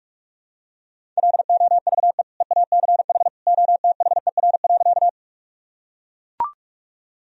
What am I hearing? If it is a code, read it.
Morse code, heard as a keyed tone, about 41 wpm, 700 Hz. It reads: LOVE EACH OTHER.
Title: Sentences from Top 1000 Words - Review Part 1 - 45wpm